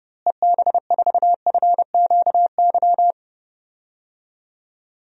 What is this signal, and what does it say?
Morse code, heard as a keyed tone, about 31 wpm, 700 Hz. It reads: EB4FQY
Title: DXCC-100 Call Signs - 6 Characters - Repeated 3x - 30wpm